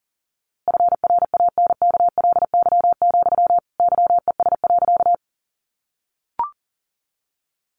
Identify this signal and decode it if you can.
Morse code, heard as a keyed tone, about 40 wpm, 700 Hz. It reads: FRANKLY, YES.
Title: A Sound Of Thunder 40wpm